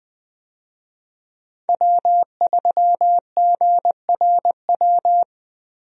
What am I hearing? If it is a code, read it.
Morse code, heard as a keyed tone, about 21 wpm, 700 Hz. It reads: W3GRW
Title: USA Call Signs - 5 Characters - Repeated 2x - 20wpm